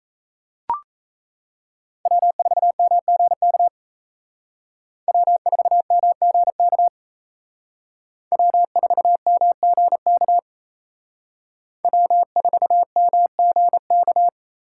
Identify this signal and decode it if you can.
Morse code, heard as a keyed tone, about 32 wpm, 700 Hz. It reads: W4MGK W4MGK W4MGK W4MGK
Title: Speed Racing - USA 1x3 Call Signs 28wpm